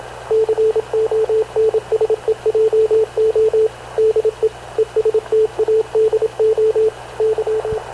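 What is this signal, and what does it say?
Morse code, heard as a keyed tone, about 27 wpm, 440 Hz, with radio noise. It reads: CONSEJO DE ESTADO Y
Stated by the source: Havana Radio CLA press (PX) broadcast, recorded by Radio Officer Paul Dumesnil aboard mv CNAV QUEST/CZDO.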